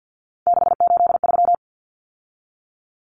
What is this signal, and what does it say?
Morse code, heard as a keyed tone, about 50 wpm, 700 Hz. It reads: 683
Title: Numbers - 3-Digit 50wpm